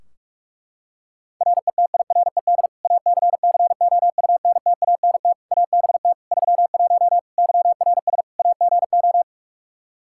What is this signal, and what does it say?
Morse code, heard as a keyed tone, about 41 wpm, 700 Hz. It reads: RETIRED ACCOUNTANT ABT 31 YRS AGO
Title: K1USN SST QSO Practice 45wpm